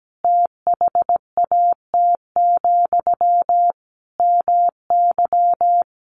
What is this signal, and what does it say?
Morse code, heard as a keyed tone, about 17 wpm, 700 Hz. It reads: THAT, MY
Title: Sentences from Top 1500 Words 17wpm